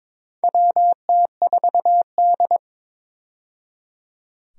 Morse code, heard as a keyed tone, about 22 wpm, 700 Hz. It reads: WT4D
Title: USA Call Signs - 4 Characters - 22wpm